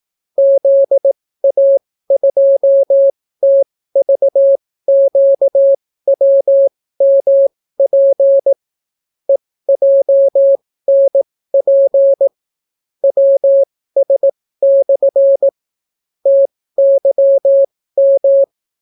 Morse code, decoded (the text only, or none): ZA2TVQWMP EJNP WS/ TYM